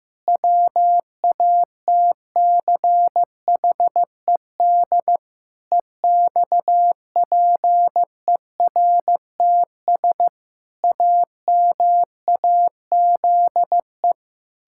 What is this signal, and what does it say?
Morse code, heard as a keyed tone, about 15 wpm, 700 Hz. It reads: WATCHED EXPERTS AMAZE
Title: Pangram Challenge v2 - 15wpm